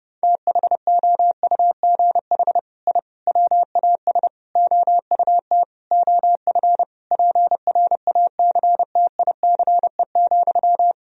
Morse code, heard as a keyed tone, about 30 wpm, 700 Hz. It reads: THOUGH I WAS OUT OF PRACTICE,
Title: Sentences from Top 1000 Words - Review Part 4 - 30wpm